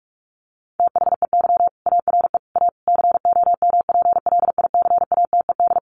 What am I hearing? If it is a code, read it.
Morse code, heard as a keyed tone, about 44 wpm, 700 Hz. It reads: THEY ARE A COMPLICATED
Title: Sentences from Top 1200 Words 45wpm